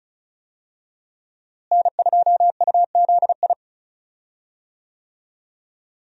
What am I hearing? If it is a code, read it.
Morse code, heard as a keyed tone, about 36 wpm, 700 Hz. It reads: N2UZI
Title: USA Call Signs - 5 Characters - Repeated 2x - 35wpm